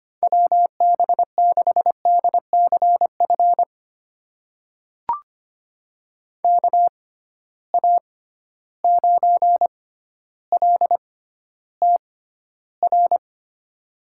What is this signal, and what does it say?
Morse code, heard as a keyed tone, about 25 wpm, spaced as for 13 wpm, 700 Hz. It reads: WB6DCF KA9LTR
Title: USA Call Signs - 6 Characters - 6x Character Spacing - 25wpm